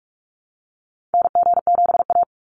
Morse code, heard as a keyed tone, about 46 wpm, 700 Hz. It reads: NG7A
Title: NA SOTA Activator Callsigns 45wpm